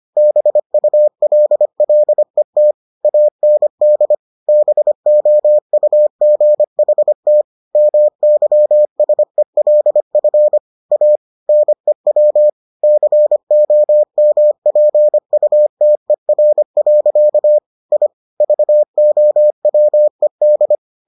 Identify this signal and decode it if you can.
Morse code, heard as a keyed tone, about 25 wpm, 600 Hz. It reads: BULLET AND BOUGHT MYSELF A NEW COMPUTER. I VOWED